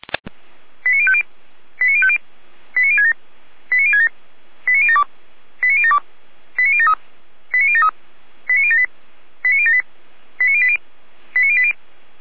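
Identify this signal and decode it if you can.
Five-tone selective call (ZVEI), heard as quick runs of five tones, tones of about 70 ms. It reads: 89050 89050 89067 89067 89072 89072 89073 89073 89078 89078 89080 89080
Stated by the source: ZVEI five-tone call received on SDR.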